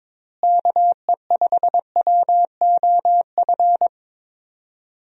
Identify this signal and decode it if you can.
Morse code, heard as a keyed tone, about 21 wpm, 700 Hz. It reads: KE5WOF
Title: USA Call Signs - 6 Characters - Repeated 3x - 22wpm